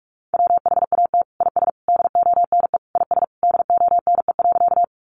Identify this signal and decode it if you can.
Morse code, heard as a keyed tone, about 44 wpm, 700 Hz. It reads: WHAT IS DONE IS DONE.
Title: Sentences from Top 500 Words 45wpm